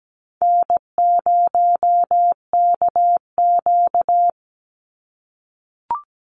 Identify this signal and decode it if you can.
Morse code, heard as a keyed tone, about 17 wpm, 700 Hz. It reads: N0KQ
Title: Easy Call Signs - 17wpm